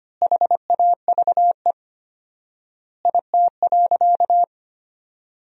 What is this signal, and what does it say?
Morse code, heard as a keyed tone, about 25 wpm, 700 Hz. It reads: HAVE IT.
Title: 4x Word-Spacing - Sentences from Top 300 Words - 25wpm